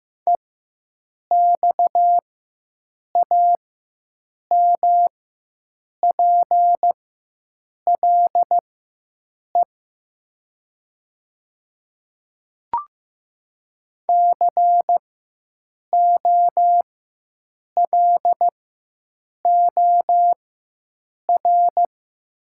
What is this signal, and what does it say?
Morse code, heard as a keyed tone, about 15 wpm, spaced as for 7 wpm, 700 Hz. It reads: EXAMPLE COLOR
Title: 4x Character Spacing - Top 300 Words - 15wpm